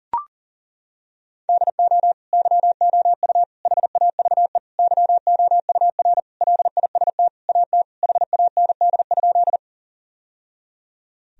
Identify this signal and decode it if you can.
Morse code, heard as a keyed tone, about 40 wpm, 700 Hz. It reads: DO YOU HAVE YOUR LIST AT HAND?
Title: Sentences from Top 1000 Words - Review Part 2 - 40wpm